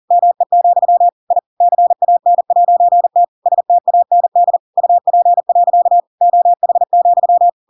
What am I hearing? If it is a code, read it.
Morse code, heard as a keyed tone, about 40 wpm, 700 Hz. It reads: ME, I CAN'T STAND UP. OH,
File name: Pinocchio / Pinocchio ch7 40 40 ik1hge